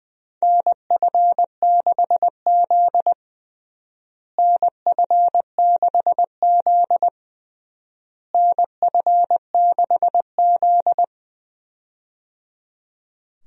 Morse code, heard as a keyed tone, about 21 wpm, 700 Hz. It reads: NF6Z NF6Z NF6Z
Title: USA Call Signs - 4 Characters - Repeated 3x - 20wpm